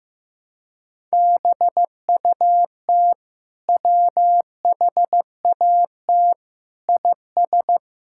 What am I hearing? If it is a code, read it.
Morse code, heard as a keyed tone, about 15 wpm, 700 Hz. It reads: BUT WHAT IS
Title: Sentences from Top 1000 Words - Review Part 6 - 15wpm